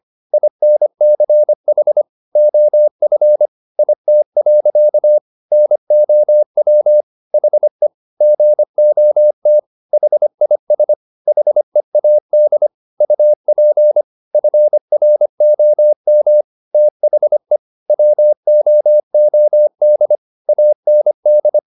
Morse code, heard as a keyed tone, about 25 wpm, 600 Hz. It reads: INCH OF IT. NOW HE GOT HIS HEAD UP FROM THE WOOD AND